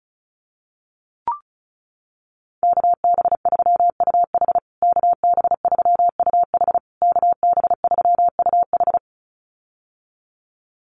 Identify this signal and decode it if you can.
Morse code, heard as a keyed tone, about 34 wpm, 700 Hz. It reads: KB3UH KB3UH KB3UH
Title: USA Call Signs Repeated 3x 35wpm